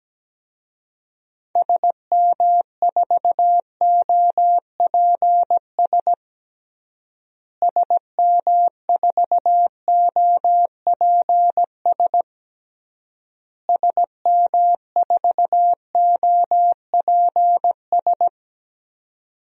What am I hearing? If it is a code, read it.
Morse code, heard as a keyed tone, about 17 wpm, 700 Hz. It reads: SM4OPS SM4OPS SM4OPS
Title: DXCC-100 Call Signs - 6 Characters - Repeated 3x - 17wpm